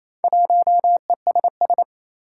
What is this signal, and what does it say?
Morse code, heard as a keyed tone, about 29 wpm, 700 Hz. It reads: 1ESS
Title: Four Challenging Letter-Number Combinations 28wpm